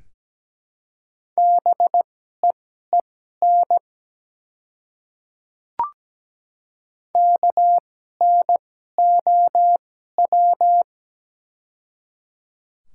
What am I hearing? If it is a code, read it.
Morse code, heard as a keyed tone, about 17 wpm, spaced as for 12 wpm, 700 Hz. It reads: BEEN KNOW
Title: 2x Character Spacing - Top 100 Words - 17wpm